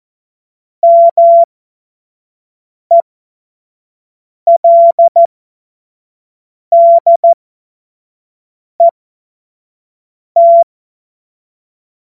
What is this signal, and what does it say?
Morse code, heard as a keyed tone, about 14 wpm, spaced as for 5 wpm, 680 Hz. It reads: MELDET
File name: Morsekurs / Witze / Witz 31.Oktober 2023 25 F70